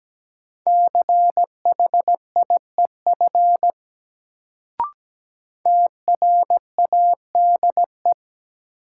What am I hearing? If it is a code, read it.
Morse code, heard as a keyed tone, about 17 wpm, 700 Hz. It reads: CHIEF TRADE
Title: Sets of 5 Words Encore - New Word Drill - 17wpm